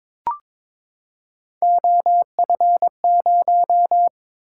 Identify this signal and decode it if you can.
Morse code, heard as a keyed tone, about 23 wpm, 700 Hz. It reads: OF0
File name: practice / 22wpm / Three Letter-Number Combinations - 22wpm